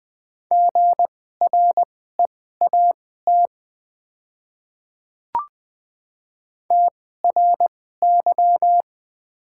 Morse code, heard as a keyed tone, about 20 wpm, spaced as for 14 wpm, 700 Hz. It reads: GREAT TRY
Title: 2x Character Spacing - Top 300 Words - 20wpm